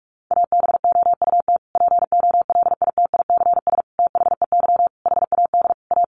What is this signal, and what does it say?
Morse code, heard as a keyed tone, about 44 wpm, 700 Hz. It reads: ABOUT POLITICS THEY HAD A
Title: Sentences from Top 1000 Words 45wpm